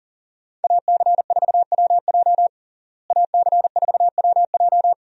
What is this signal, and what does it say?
Morse code, heard as a keyed tone, about 37 wpm, 700 Hz. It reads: AC4WJ AC4WJ
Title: USA Call Signs - 5 Characters - Repeated 2x - 40wpm